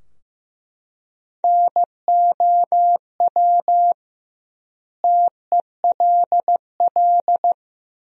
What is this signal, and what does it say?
Morse code, heard as a keyed tone, about 15 wpm, 700 Hz. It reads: NOW TELL
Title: 2x Word-Spacing - Sentences from Top 200 Words - 15wpm